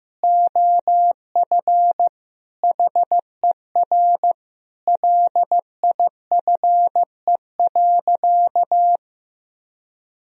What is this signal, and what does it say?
Morse code, heard as a keyed tone, about 15 wpm, 700 Hz. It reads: OF HER LIFE.
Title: Sentences from Top 1000 Words - Review Part 3 - 15wpm